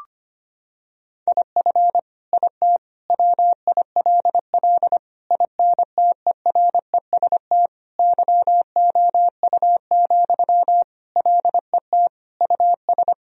Morse code, heard as a keyed tone, about 25 wpm, 700 Hz. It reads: IF IT WILL INTEREST YOU, LET US
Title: Sentences from Top 1000 Words - Review Part 2 - 25wpm